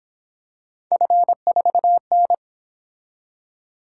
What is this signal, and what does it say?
Morse code, heard as a keyed tone, about 26 wpm, 700 Hz. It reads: F4N
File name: Speed-Racing - Three Letter-Number Combinations - 22wpm